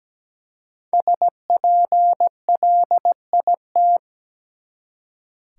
Morse code, heard as a keyed tone, about 17 wpm, 700 Hz. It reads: SPLIT
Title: Five Letter Words - 17wpm